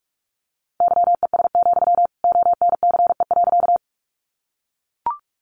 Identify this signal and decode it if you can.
Morse code, heard as a keyed tone, about 44 wpm, 700 Hz. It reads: YES, ONCE.
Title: Sentences from Top 500 Words 45wpm